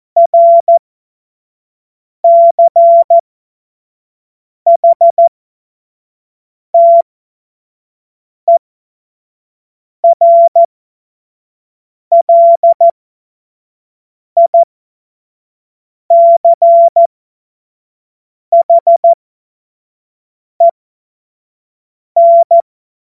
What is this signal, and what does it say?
Morse code, heard as a keyed tone, about 14 wpm, spaced as for 5 wpm, 680 Hz. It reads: RCHTERLICHEN
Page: cw-witze